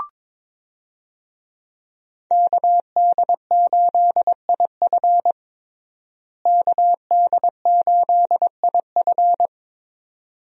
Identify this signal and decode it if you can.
Morse code, heard as a keyed tone, about 22 wpm, 700 Hz. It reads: KD8IF KD8IF
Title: USA Call Signs - 5 Characters - Repeated 2x - 22wpm